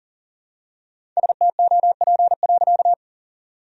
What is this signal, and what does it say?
Morse code, heard as a keyed tone, about 40 wpm, 700 Hz. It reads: STOP.
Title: Sentences from Top 1000 Words - Review Part 1 - 40wpm